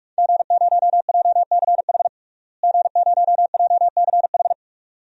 Morse code, heard as a keyed tone, about 46 wpm, 700 Hz. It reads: G0JCH G0JCH
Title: DXCC-100 Call Signs - 5 Characters - Repeated 2x - 45wpm